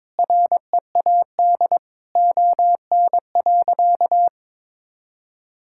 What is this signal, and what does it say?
Morse code, heard as a keyed tone, about 22 wpm, 700 Hz. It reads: READ ON.
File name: Sentences with 9-Word Phrases - 22wpm